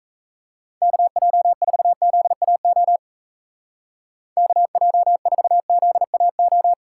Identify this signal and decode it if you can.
Morse code, heard as a keyed tone, about 42 wpm, 700 Hz. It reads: KJ4ZAO KJ4ZAO